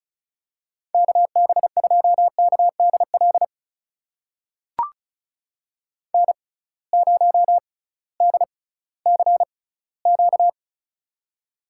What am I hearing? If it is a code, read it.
Morse code, heard as a keyed tone, about 36 wpm, spaced as for 19 wpm, 700 Hz. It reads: KB2KDL N0DCQ
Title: USA Call Signs - 6x Character Spacing - 35wpm